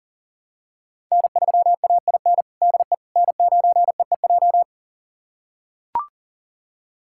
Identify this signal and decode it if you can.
Morse code, heard as a keyed tone, about 37 wpm, 700 Hz. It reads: N3AIN DE N9EEJ
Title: USA Callsign DE Callsign 40wpm